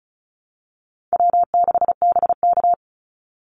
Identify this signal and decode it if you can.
Morse code, heard as a keyed tone, about 34 wpm, 700 Hz. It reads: W6BK
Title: Call Signs - 35wpm